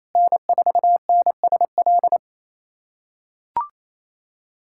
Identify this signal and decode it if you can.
Morse code, heard as a keyed tone, about 28 wpm, 700 Hz. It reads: N4NSL
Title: USA Callsign DE Callsign - 5 Characters - 28wpm